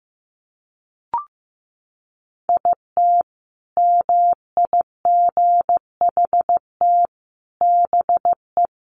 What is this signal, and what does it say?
Morse code, heard as a keyed tone, about 15 wpm, 700 Hz. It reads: IT MIGHT BE
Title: Sentences from Top 1200 Words 15wpm